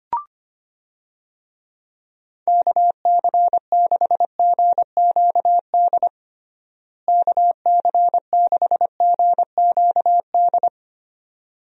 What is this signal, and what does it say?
Morse code, heard as a keyed tone, about 25 wpm, 700 Hz. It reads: KC6GQD KC6GQD
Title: USA Call Signs - 6 Characters - Repeated 2x - 25wpm